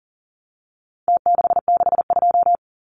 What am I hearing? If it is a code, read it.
Morse code, heard as a keyed tone, about 40 wpm, 700 Hz. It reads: T662
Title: Numbers - Contest Style - 4-Digit 40wpm